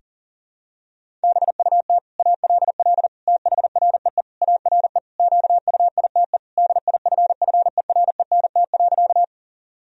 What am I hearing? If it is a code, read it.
Morse code, heard as a keyed tone, about 40 wpm, 700 Hz. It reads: BUT ALL THREE ARE QUITE DIFFERENT.
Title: Sentences from Top 1000 Words - Review Part 3 - 40wpm